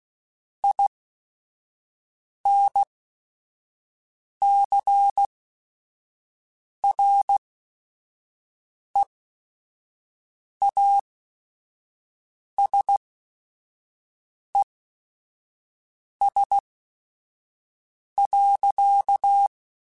Morse code, heard as a keyed tone, about 16 wpm, spaced as for 5 wpm, 800 Hz. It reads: INCREASES.
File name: Technical Subjects / Technical Subjects-mp3s / XL @5